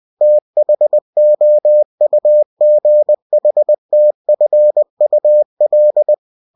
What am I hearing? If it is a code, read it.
Morse code, heard as a keyed tone, about 20 wpm, 600 Hz. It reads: THOUGHTFUL
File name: cw-books / winnie-the-pooh / winnie-20wpm-page-0025